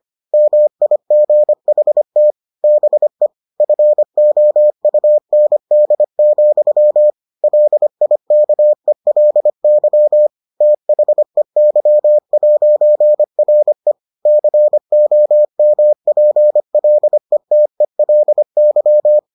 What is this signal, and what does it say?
Morse code, heard as a keyed tone, about 25 wpm, 600 Hz. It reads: MIGHT BE FOUND, LIKELY THEY'RE COMPLETELY